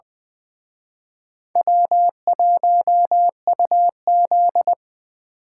Morse code, heard as a keyed tone, about 23 wpm, 700 Hz. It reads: W1UZ
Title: Speed Racing - USA Call Signs 20wpm